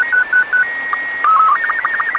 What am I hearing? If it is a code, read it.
SSTV (slow-scan television), Robot 36.